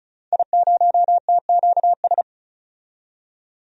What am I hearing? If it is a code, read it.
Morse code, heard as a keyed tone, about 36 wpm, 700 Hz. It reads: I0TQS
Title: DXCC-100 Call Signs - 5 Characters - 35wpm